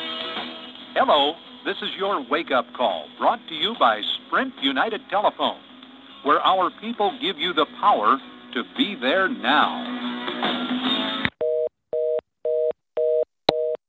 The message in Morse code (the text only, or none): none